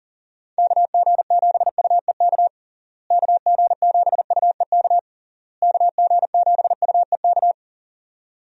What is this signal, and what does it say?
Morse code, heard as a keyed tone, about 37 wpm, 700 Hz. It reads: KG7UEK KG7UEK KG7UEK
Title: USA Call Signs - 6 Characters - Repeated 3x - 40wpm